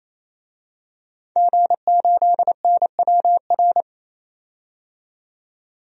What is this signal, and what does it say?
Morse code, heard as a keyed tone, about 27 wpm, 700 Hz. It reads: G8NWR
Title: DXCC-100 Call Signs - 5 Characters - 28wpm